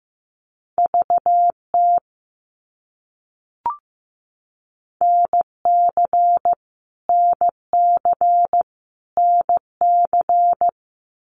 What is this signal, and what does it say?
Morse code, heard as a keyed tone, about 15 wpm, 700 Hz. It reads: VT NC NC NC
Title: US States Repeated 3x 15wpm